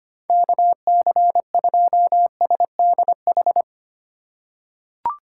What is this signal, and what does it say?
Morse code, heard as a keyed tone, about 25 wpm, 690 Hz. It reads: KC2SDH
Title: USA Call Signs - 6 Characters - 2x Character Spacing - 25wpm